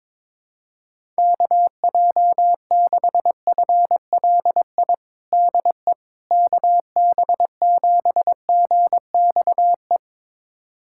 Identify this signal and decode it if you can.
Morse code, heard as a keyed tone, about 21 wpm, 700 Hz. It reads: KJ6FLI DE KB7GXE
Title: USA Callsign DE Callsign - 6 Characters - 22wpm